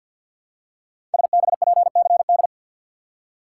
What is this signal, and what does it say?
Morse code, heard as a keyed tone, about 50 wpm, 700 Hz. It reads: SBPCD
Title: Code Groups - 5 Letters - 50wpm